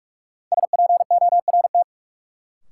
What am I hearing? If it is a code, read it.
Morse code, heard as a keyed tone, about 47 wpm, 700 Hz. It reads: SPORT
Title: Sentences from Top 1500 Words - New Word Drill Alt - 45wpm